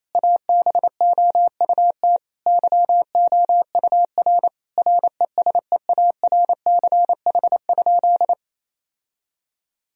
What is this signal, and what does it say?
Morse code, heard as a keyed tone, about 28 wpm, 700 Hz. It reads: ABOUT YOUR RESEARCH?
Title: Sentences from Top 1000 Words - Review Part 4 - 28wpm